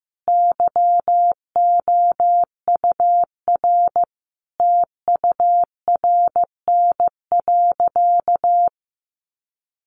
Morse code, heard as a keyed tone, about 15 wpm, 700 Hz. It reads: YOUR TURN.
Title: Sentences from Top 600 Words 15wpm